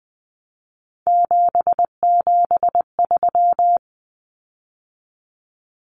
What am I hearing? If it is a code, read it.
Morse code, heard as a keyed tone, about 20 wpm, 700 Hz. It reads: 773
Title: Common Cut Numbers - 3-Digit 20wpm